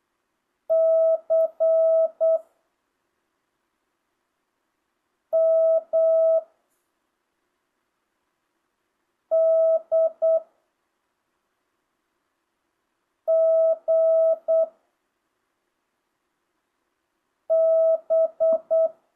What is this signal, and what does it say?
Morse code, heard as a keyed tone, about 8 wpm, 660 Hz. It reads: C M D G B
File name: jh8cbh / 2025cw